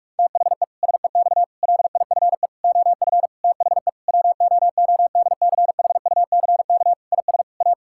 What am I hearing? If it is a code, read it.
Morse code, heard as a keyed tone, about 41 wpm, 700 Hz. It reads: THE SEX LIFE OF THE WOODCHUCK IS A
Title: Pangram Challenge v2 - 45wpm